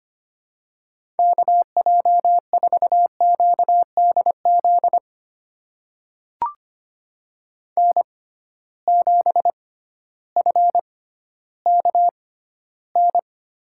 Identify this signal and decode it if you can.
Morse code, heard as a keyed tone, about 25 wpm, spaced as for 14 wpm, 700 Hz. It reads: KJ4QDZ N7FKN
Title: USA Call Signs - 6x Character Spacing - 25wpm